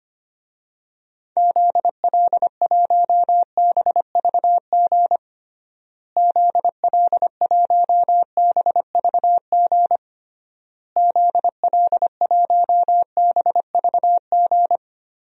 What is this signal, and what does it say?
Morse code, heard as a keyed tone, about 25 wpm, 700 Hz. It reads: ZL1BVG ZL1BVG ZL1BVG
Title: DXCC-100 Call Signs - 6 Characters - Repeated 3x - 25wpm